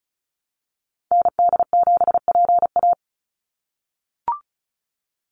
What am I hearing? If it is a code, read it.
Morse code, heard as a keyed tone, about 36 wpm, 700 Hz. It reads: ND7PA
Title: NA SOTA Activator Callsigns 35wpm